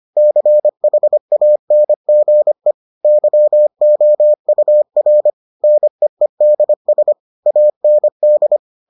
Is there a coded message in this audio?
Yes: Morse code from 0.2 to 8.6 s, heard as a keyed tone, about 25 wpm, 600 Hz. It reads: CHANGE YOUR NEEDS AND